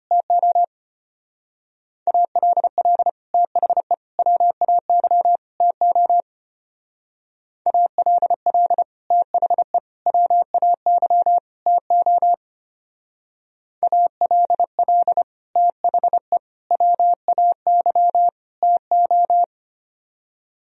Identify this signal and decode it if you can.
Morse code, heard as a keyed tone, about 29 wpm, 700 Hz. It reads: TO ALL THE WAY TO ALL THE WAY TO ALL THE WAY TO
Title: Speed-Racing - Sets of 4 Words 25wpm